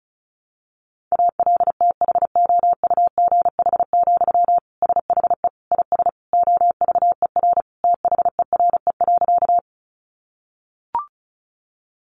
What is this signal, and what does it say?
Morse code, heard as a keyed tone, about 34 wpm, 700 Hz. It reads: ALTHOUGH, SHE IS OVER THERE.
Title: Sentences from Top 600 Words 35wpm